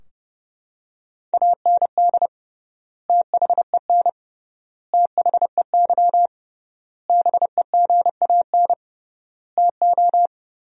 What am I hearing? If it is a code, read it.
Morse code, heard as a keyed tone, about 30 wpm, 700 Hz. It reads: AND THEN THEY BEGAN TO
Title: 3x Word-Spacing - Sentences from Top 300 Words - 30wpm